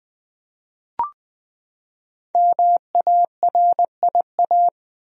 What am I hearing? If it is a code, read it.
Morse code, heard as a keyed tone, about 21 wpm, 700 Hz. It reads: MARIA
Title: USA Common Names - Max 5 Letters - 20wpm